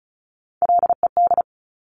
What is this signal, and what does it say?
Morse code, heard as a keyed tone, about 36 wpm, 700 Hz. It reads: LED
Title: Top Three Letter Combinations in Words 35wpm